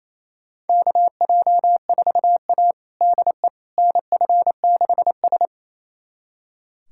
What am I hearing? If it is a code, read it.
Morse code, heard as a keyed tone, about 27 wpm, 700 Hz. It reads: KJ4A DE NF6S
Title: USA Callsign DE Callsign - 4 Characters - 28wpm